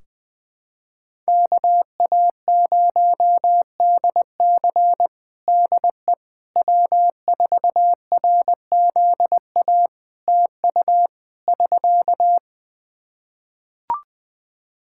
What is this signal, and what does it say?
Morse code, heard as a keyed tone, about 20 wpm, 700 Hz. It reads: KA0DC DE W4RZA TU <SK>
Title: Practice QSO - Exchange 4 - Ending - 20wpm